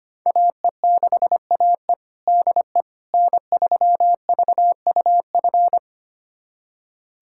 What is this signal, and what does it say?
Morse code, heard as a keyed tone, about 25 wpm, 700 Hz. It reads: AE6AE DE N3VUF
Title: USA Callsign DE Callsign - 5 Characters - 25wpm